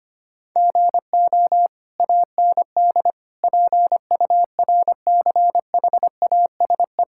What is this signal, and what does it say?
Morse code, heard as a keyed tone, about 25 wpm, 700 Hz. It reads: GO AND PURCHASE
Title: Sentences from Top 1000 Words - Review Part 5 - 25wpm